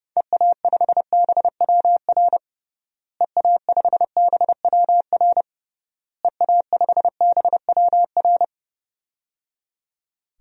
Morse code, heard as a keyed tone, about 30 wpm, 700 Hz. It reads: EA5BWR EA5BWR EA5BWR
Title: DXCC-100 Call Signs - 6 Characters - Repeated 3x - 30wpm